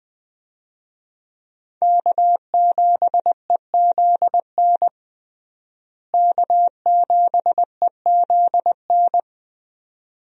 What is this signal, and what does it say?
Morse code, heard as a keyed tone, about 20 wpm, 700 Hz. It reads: K7EZN K7EZN
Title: USA Call Signs - 5 Characters - Repeated 2x - 20wpm